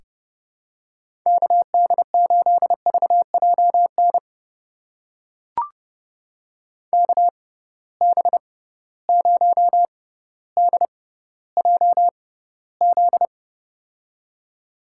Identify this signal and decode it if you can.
Morse code, heard as a keyed tone, about 30 wpm, spaced as for 15 wpm, 700 Hz. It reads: KD8VJN KB0DJZ
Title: USA Call Signs - 6 Characters - 6x Character Spacing - 30wpm